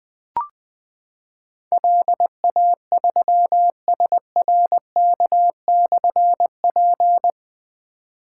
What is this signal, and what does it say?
Morse code, heard as a keyed tone, about 20 wpm, 700 Hz. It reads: LA3SRK/P
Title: Difficult Call Signs 20wpm